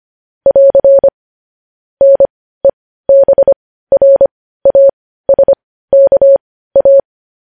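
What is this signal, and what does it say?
Morse code, heard as a keyed tone, about 25 wpm, spaced as for 15 wpm, 550 Hz. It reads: <AR> NEBRASKA
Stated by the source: Morse code news headlines at 15 wpm.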